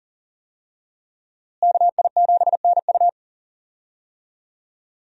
Morse code, heard as a keyed tone, about 40 wpm, 700 Hz. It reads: KI7NU
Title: USA Call Signs - 5 Characters - Repeated 3x - 40wpm